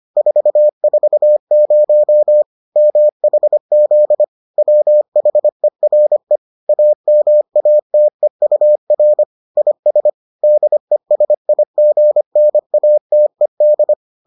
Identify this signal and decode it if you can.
Morse code, heard as a keyed tone, about 25 wpm, 600 Hz. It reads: 440 MHZ WHERE AMATEUR IS DESIGNATED